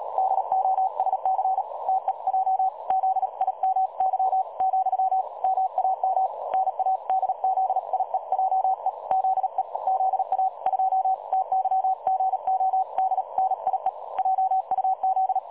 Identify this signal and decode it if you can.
Morse code, heard as a keyed tone, about 38 wpm, 760 Hz. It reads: DOSC TEJ ZIMY , MAM NADZIEJE ZE ZA 2 TYGODNIE JUZ
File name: dlazaawansowanych / nagraniazpasma / websdr recording 3565.0kHz